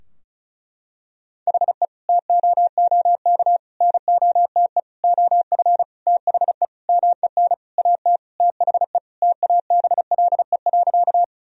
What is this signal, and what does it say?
Morse code, heard as a keyed tone, about 36 wpm, 700 Hz. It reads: HE TOOK NOTE OF THE MEN AT THE TABLE.
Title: Sentences from Top 1000 Words - Review Part 2 - 35wpm